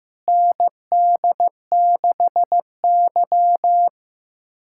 Morse code, heard as a keyed tone, about 15 wpm, 700 Hz. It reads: ND6Y
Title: USA Callsign DE Callsign - 4 Characters - 15wpm